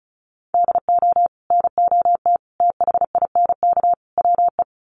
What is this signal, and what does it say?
Morse code, heard as a keyed tone, about 34 wpm, 700 Hz. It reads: DO NOT THINK WE
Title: Sets of 4 Words 35wpm